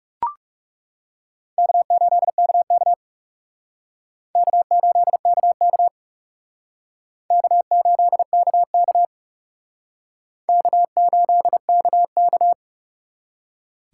Morse code, heard as a keyed tone, about 39 wpm, 700 Hz. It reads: K8KK K8KK K8KK K8KK